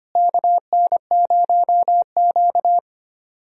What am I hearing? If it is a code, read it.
Morse code, heard as a keyed tone, about 25 wpm, 700 Hz. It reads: KN0Q
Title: USA Call Signs - 4 Characters - Repeated 3x - 25wpm